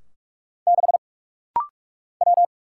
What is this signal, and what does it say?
Morse code, heard as a keyed tone, about 46 wpm, 700 Hz. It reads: 6 W
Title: Single Letter-Number Alt - Rapid-Fire 45wpm